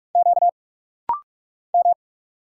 Morse code, heard as a keyed tone, about 46 wpm, 700 Hz. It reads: Q M
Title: Single Letters - Rapid-Fire 45wpm